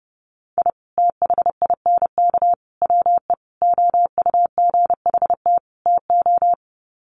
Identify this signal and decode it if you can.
Morse code, heard as a keyed tone, about 30 wpm, 700 Hz. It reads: I THINK WE OUGHT TO
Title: Sets of 5 Words Encore 30wpm